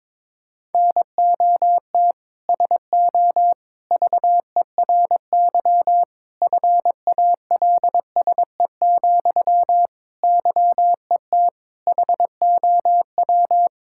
Morse code, heard as a keyed tone, about 22 wpm, 700 Hz. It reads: NOT SO VERY FALSE, YET HOW
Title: Sentences from Top 1600 Words 22wpm